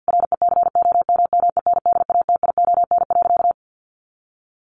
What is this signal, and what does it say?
Morse code, heard as a keyed tone, about 50 wpm, 700 Hz. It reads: RECOMMENDATION.
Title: Sentences from Top 1400 Words 50wpm